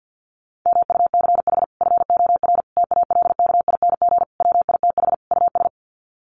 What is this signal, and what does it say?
Morse code, heard as a keyed tone, about 50 wpm, 700 Hz. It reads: MUCH FOR TALKING WITH US
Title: Sets of 5 Words Encore 50wpm